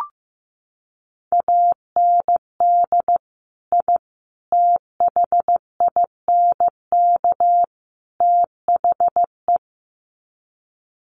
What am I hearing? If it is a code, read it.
Morse code, heard as a keyed tone, about 15 wpm, 700 Hz. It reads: AND I THINK THE
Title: Sets of 4 Words 15wpm